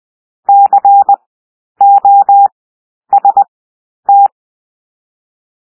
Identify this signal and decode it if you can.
Morse code, heard as a keyed tone, about 20 wpm, 800 Hz. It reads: C O S T
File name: cwmp3 / W1 SOUND WORDS 20 wpm